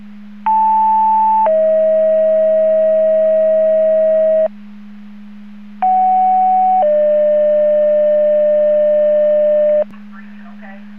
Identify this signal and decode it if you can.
Two-tone page, heard as a one-second tone, then a three-second tone, about 880 then 640 Hz, about 750 then 600 Hz, with speech. It is heard from 0.5 to 9.8 s.